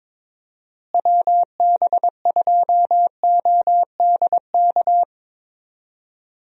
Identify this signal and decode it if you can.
Morse code, heard as a keyed tone, about 22 wpm, 700 Hz. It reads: WB2ODK
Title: USA Call Signs - 6 Characters - Repeated 2x - 22wpm